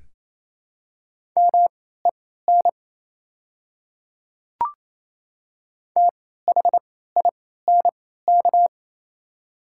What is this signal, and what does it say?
Morse code, heard as a keyed tone, about 28 wpm, spaced as for 16 wpm, 700 Hz. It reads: MEN THINK
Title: 3x Character Spacing - Top 200 Words - 28wpm